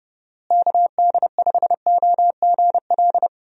Morse code, heard as a keyed tone, about 30 wpm, 700 Hz. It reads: KD5OGL
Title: USA Call Signs - 6 Characters - Repeated 3x - 30wpm